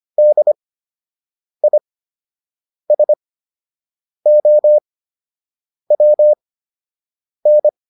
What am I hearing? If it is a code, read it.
Morse code, heard as a keyed tone, about 25 wpm, spaced as for 7 wpm, 600 Hz. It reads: DISOWN